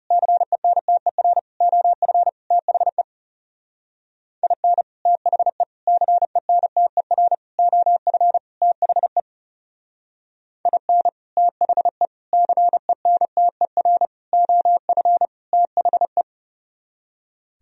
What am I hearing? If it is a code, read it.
Morse code, heard as a keyed tone, about 36 wpm, 700 Hz. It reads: CENTER OF THE IN THE CENTER OF THE IN THE CENTER OF THE
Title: Speed-Racing - Sets of 5 Words 30wpm